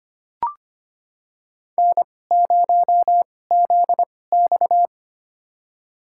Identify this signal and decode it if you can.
Morse code, heard as a keyed tone, about 25 wpm, spaced as for 18 wpm, 700 Hz. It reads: N0ZX
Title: USA Call Signs - 2x Character Spacing - 25wpm